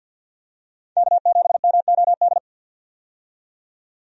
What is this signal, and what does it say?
Morse code, heard as a keyed tone, about 51 wpm, 700 Hz. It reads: K7MOD
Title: USA Call Signs - 5 Characters - 50wpm